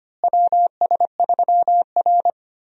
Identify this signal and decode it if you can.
Morse code, heard as a keyed tone, about 25 wpm, 700 Hz. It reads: WS3R
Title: USA Call Signs - 4 Characters - Repeated 3x - 25wpm